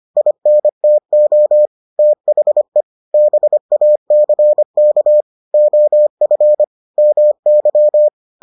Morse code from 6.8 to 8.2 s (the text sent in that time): MY